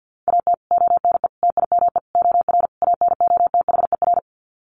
Morse code, heard as a keyed tone, about 50 wpm, 700 Hz. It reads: AT ONE TIME OR ANOTHER
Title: Sets of 5 Words Encore 50wpm